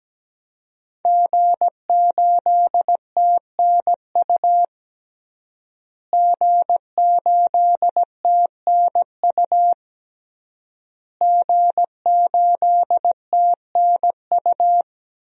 Morse code, heard as a keyed tone, about 17 wpm, 700 Hz. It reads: G8TNU G8TNU G8TNU
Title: DXCC-100 Call Signs - 5 Characters - Repeated 3x - 17wpm